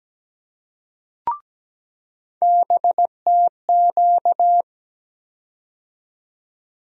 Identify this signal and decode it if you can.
Morse code, heard as a keyed tone, about 17 wpm, 700 Hz. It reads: BTQ
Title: USA Call Sign Postfixes - 3 Letters 17wpm